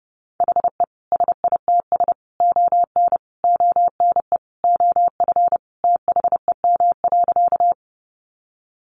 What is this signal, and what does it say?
Morse code, heard as a keyed tone, about 30 wpm, 700 Hz. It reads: HE SITS ON ONE OF THEM.
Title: Additional Easy Phrases 30wpm